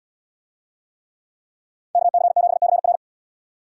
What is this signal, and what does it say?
Morse code, heard as a keyed tone, about 69 wpm, 700 Hz. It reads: KC7CR